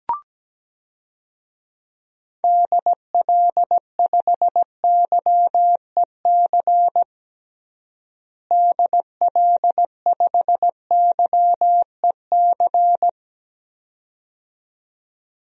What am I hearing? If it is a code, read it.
Morse code, heard as a keyed tone, about 17 wpm, 700 Hz. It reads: DL5YEC DL5YEC
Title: DXCC-100 Call Signs - 6 Characters - Repeated 2x - 17wpm